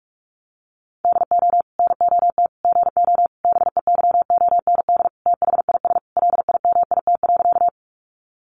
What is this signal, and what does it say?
Morse code, heard as a keyed tone, about 44 wpm, 700 Hz. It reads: DO NOT GO BEYOND THIS LIMIT.
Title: Sentences from Top 1500 Words 45wpm